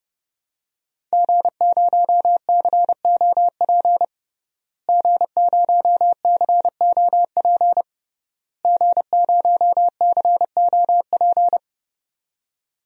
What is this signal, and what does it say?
Morse code, heard as a keyed tone, about 30 wpm, 700 Hz. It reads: G0COP G0COP G0COP
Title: DXCC-100 Call Signs - 5 Characters - Repeated 3x - 30wpm